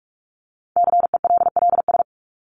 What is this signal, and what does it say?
Morse code, heard as a keyed tone, about 44 wpm, 700 Hz. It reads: CELLS
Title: ES:S Suffix - Top 100 Words with Base Words 45wpm